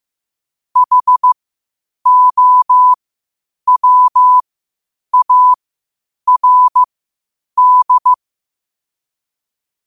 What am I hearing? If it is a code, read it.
Morse code, heard as a keyed tone, about 15 wpm, spaced as for 8 wpm, 1000 Hz. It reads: HOWARD